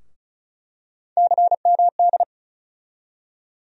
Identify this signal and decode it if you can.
Morse code, heard as a keyed tone, about 36 wpm, 700 Hz. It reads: CMD
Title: USA Call Sign Postfixes - 3 Letters 35wpm